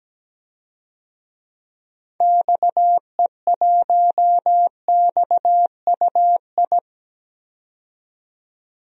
Morse code, heard as a keyed tone, about 17 wpm, 700 Hz. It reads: XE1XUI
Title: DXCC-100 Call Signs - 6 Characters - 17wpm